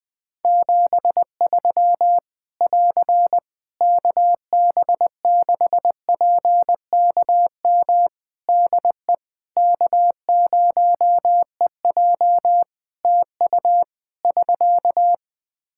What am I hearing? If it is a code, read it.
Morse code, heard as a keyed tone, about 20 wpm, 700 Hz. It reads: 73 <AR> KB6PKM DE K0EJ TU <SK>
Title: Practice QSO - Exchange 4 - Ending - 20wpm